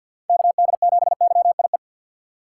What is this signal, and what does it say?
Morse code, heard as a keyed tone, about 47 wpm, 700 Hz. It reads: KD7YIE
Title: USA Call Signs - 6 Characters - 3x Character Spacing - 50wpm